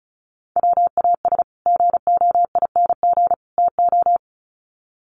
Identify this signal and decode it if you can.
Morse code, heard as a keyed tone, about 34 wpm, 700 Hz. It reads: WAS GOING TO
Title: Sets of 3 Words 35wpm